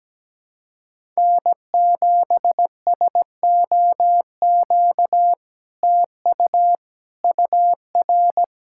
Morse code, heard as a keyed tone, about 17 wpm, 700 Hz. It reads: N7SOQ TU UR
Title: SOTA QSO Practice 17wpm